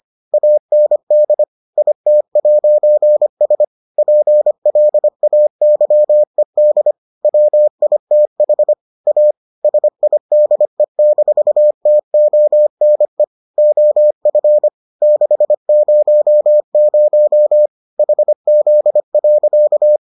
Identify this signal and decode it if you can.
Morse code, heard as a keyed tone, about 25 wpm, 600 Hz. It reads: AND IT'S PLAYED WITH A SIDE-TONE OF 600 HZ.